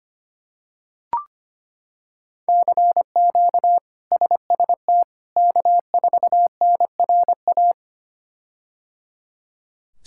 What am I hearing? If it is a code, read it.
Morse code, heard as a keyed tone, about 25 wpm, 700 Hz. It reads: CQ SST K4NRA
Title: K1USN SST QSO Practice - 25wpm